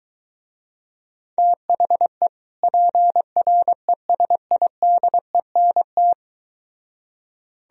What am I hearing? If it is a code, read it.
Morse code, heard as a keyed tone, about 23 wpm, 700 Hz. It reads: THE PRESIDENT